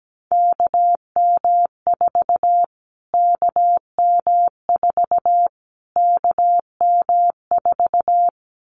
Morse code, heard as a keyed tone, about 17 wpm, 700 Hz. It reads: KM4 KM4 KM4
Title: Callsign Prefixes for Top 25 DXCC - Repeated 3x - 17wpm